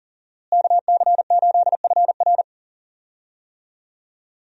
Morse code, heard as a keyed tone, about 40 wpm, 700 Hz. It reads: KC8FR
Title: USA Call Signs - 5 Characters - Repeated 2x - 40wpm